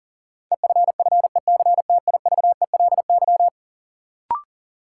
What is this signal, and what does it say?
Morse code, heard as a keyed tone, about 40 wpm, 700 Hz. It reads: EFFECTIVELY
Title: Sentences from Top 1400 Words - New Word Drill Alt - 40wpm